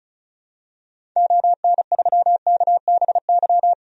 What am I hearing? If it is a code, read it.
Morse code, heard as a keyed tone, about 36 wpm, 700 Hz. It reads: ON3KBY
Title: DXCC-100 Call Signs - 6 Characters - 35wpm